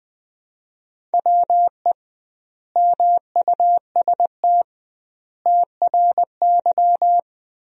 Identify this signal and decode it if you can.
Morse code, heard as a keyed tone, about 21 wpm, 700 Hz. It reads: WE MUST TRY
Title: 2x Word-Spacing - Sentences from Top 400 Words - 20wpm